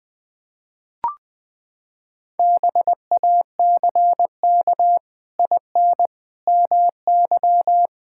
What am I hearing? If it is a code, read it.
Morse code, heard as a keyed tone, about 20 wpm, 700 Hz. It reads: BACK IN MY
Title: Pangram Challenge v2 - 20wpm